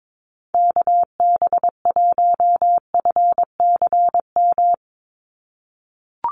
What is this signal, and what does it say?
Morse code, heard as a keyed tone, about 22 wpm, 700 Hz. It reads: KB1FCM
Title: USA Call Signs 22wpm